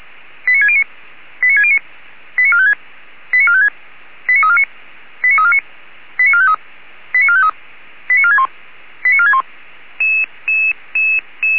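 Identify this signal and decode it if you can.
Five-tone selective call (ZVEI), heard as quick runs of five tones, tones of about 70 ms. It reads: 89609 89609 89456 89456 89349 89349 89453 89453 89461 89461
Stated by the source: ZVEI five-tone selective call 89348 received by SDR.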